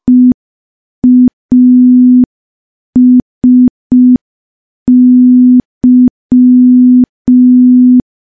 Morse code, heard as a keyed tone, about 5 wpm, 250 Hz. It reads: EASY
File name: morse / morse-training-05WPM